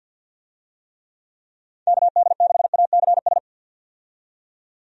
Morse code, heard as a keyed tone, about 51 wpm, 700 Hz. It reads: KD6ACS
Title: USA Call Signs - 6 Characters - 50wpm